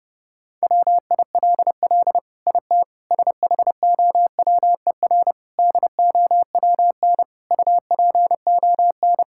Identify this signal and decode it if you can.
Morse code, heard as a keyed tone, about 32 wpm, 700 Hz. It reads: WILL IT SHOWER DOWN UPON
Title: Sentences from Top 1600 Words 30wpm